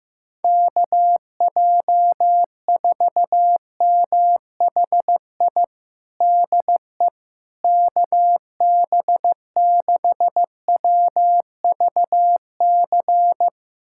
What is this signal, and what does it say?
Morse code, heard as a keyed tone, about 16 wpm, 700 Hz. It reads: KJ4MHI DE KB6WVC
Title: USA Callsign DE Callsign - 6 Characters - 15wpm